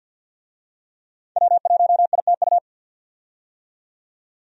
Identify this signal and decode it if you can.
Morse code, heard as a keyed tone, about 47 wpm, 700 Hz. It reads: W1ITU
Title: USA Call Signs - 5 Characters - 50wpm